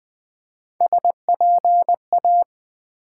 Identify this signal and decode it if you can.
Morse code, heard as a keyed tone, about 20 wpm, 700 Hz. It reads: SPA
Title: Build A Word - 1 Letter At A Time - 3 to 5-Letter Words - 20wpm